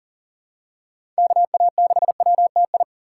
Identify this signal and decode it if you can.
Morse code, heard as a keyed tone, about 40 wpm, 700 Hz. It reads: KA6WTI
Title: USA Call Signs - 6 Characters - 2x Character Spacing - 40wpm